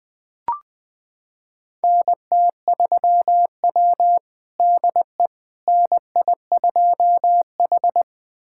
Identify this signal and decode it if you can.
Morse code, heard as a keyed tone, about 20 wpm, 700 Hz. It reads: NT3W DE NI2H
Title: USA Callsign DE Callsign - 4 Characters - 20wpm